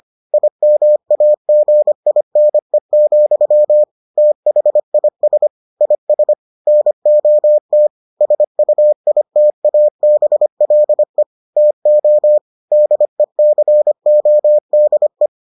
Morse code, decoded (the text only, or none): IMAGINE, THIS IS NOT SUITABLE TO DECODE